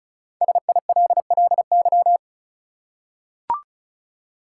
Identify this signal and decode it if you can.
Morse code, heard as a keyed tone, about 34 wpm, 700 Hz. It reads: SILLY
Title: Five Letter Words - 35wpm